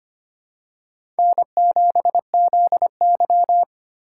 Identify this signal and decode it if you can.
Morse code, heard as a keyed tone, about 25 wpm, 700 Hz. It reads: N7ZY
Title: USA Call Signs - 4 Characters - 3x Character Spacing - 25wpm